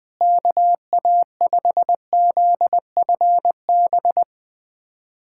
Morse code, heard as a keyed tone, about 20 wpm, 700 Hz. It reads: KA5ZFB
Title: USA Callsign DE Callsign - 6 Characters - 20wpm